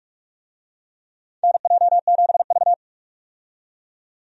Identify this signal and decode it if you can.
Morse code, heard as a keyed tone, about 46 wpm, 700 Hz. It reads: NJ7V
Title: USA Call Signs - 4 Characters - 45wpm